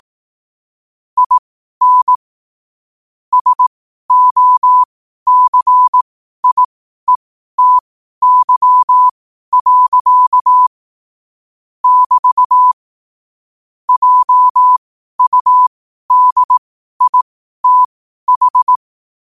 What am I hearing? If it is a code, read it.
Morse code, heard as a keyed tone, about 18 wpm, spaced as for 12 wpm, 1000 Hz. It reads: IN SOCIETY. <BT> JUDITH
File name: qotdfiles / 12wpm